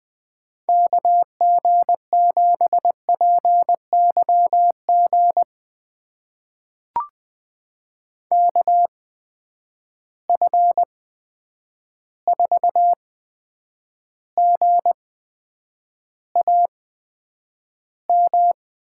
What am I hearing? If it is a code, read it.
Morse code, heard as a keyed tone, about 20 wpm, spaced as for 9 wpm, 700 Hz. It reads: KG7PYG KF4GAM
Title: USA Call Signs - 6 Characters - 8x Character Spacing - 20wpm